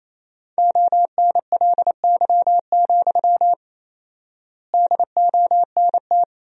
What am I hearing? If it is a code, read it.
Morse code, heard as a keyed tone, about 28 wpm, 700 Hz. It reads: ONLY, DONT
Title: 4x Word-Spacing - Sentences from Top 300 Words - 28wpm